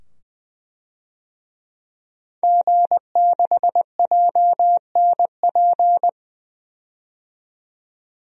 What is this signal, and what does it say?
Morse code, heard as a keyed tone, about 20 wpm, 700 Hz. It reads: G6JNP
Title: DXCC-100 Call Signs - 5 Characters - 20wpm